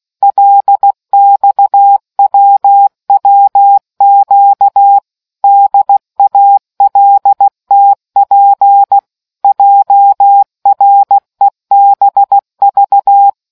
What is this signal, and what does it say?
Morse code, heard as a keyed tone, about 16 wpm, 800 Hz. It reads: LXWWQ DALTP JREBV